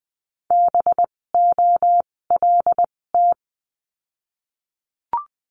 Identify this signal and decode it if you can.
Morse code, heard as a keyed tone, about 20 wpm, spaced as for 16 wpm, 700 Hz. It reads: BOLT
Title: 256 Lesson 42a - Words 12-20wpm